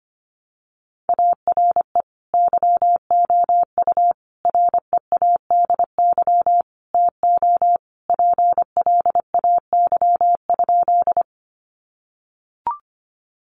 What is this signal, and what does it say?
Morse code, heard as a keyed tone, about 25 wpm, 700 Hz. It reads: ARE YOU READY TO PLAY?
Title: Sentences from Top 400 Words 25wpm